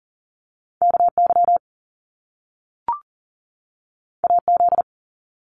Morse code, heard as a keyed tone, about 42 wpm, 700 Hz. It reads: KY AZ
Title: US States 40wpm